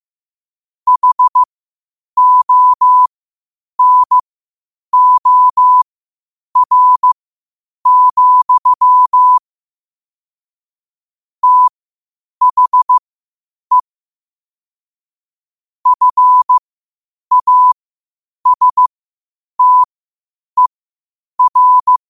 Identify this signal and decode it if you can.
Morse code, heard as a keyed tone, about 15 wpm, spaced as for 8 wpm, 1000 Hz. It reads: HONOR, THE FASTER
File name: 8wpm